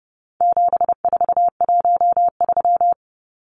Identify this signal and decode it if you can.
Morse code, heard as a keyed tone, about 28 wpm, 700 Hz. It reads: 7413
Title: Numbers - 4-Digit 30wpm